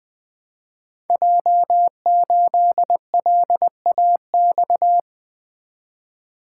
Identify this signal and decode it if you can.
Morse code, heard as a keyed tone, about 20 wpm, 700 Hz. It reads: J8LAX
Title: Code Groups - 5 Letter:Numbers - 20wpm